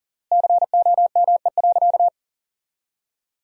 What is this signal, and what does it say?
Morse code, heard as a keyed tone, about 40 wpm, 700 Hz. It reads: COME.
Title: 3x Word-Spacing - Sentences from Top 100 Words - 40wpm